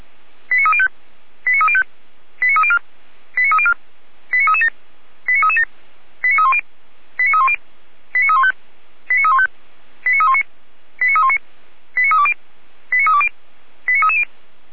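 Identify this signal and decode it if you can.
Five-tone selective call (ZVEI), heard as quick runs of five tones, tones of about 70 ms. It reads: 89305 89305 89304 89304 89337 89337 89310 89310 89315 89315 89319 89319 89320 89320 89330
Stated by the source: ZVEI five-tone selective call received by SDR, recorded by Feuerwehr Orlamünde.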